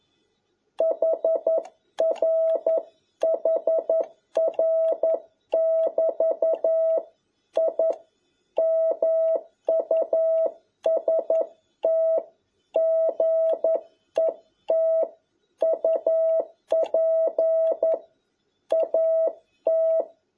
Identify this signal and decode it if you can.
Morse code, heard as a keyed tone, about 11 wpm, 650 Hz. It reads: HR HR <BT> I MUST GET UP AT
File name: jh8cbh / 2021cw